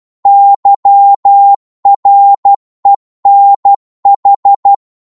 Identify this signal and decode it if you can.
Morse code, heard as a keyed tone, about 12 wpm, 800 Hz. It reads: YRENH
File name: Morse / ltr-12wpm